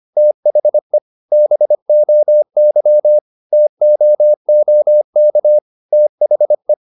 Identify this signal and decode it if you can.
Morse code, heard as a keyed tone, about 25 wpm, 600 Hz. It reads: THE BOY TOOK THE
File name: the old man and the sea part 1 25wpm